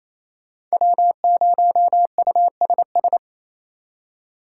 Morse code, heard as a keyed tone, about 36 wpm, 700 Hz. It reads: W0USS